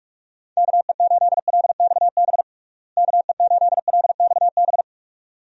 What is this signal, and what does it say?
Morse code, heard as a keyed tone, about 44 wpm, 700 Hz. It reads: KE8LXB KE8LXB
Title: USA Call Signs - 6 Characters - Repeated 2x - 45wpm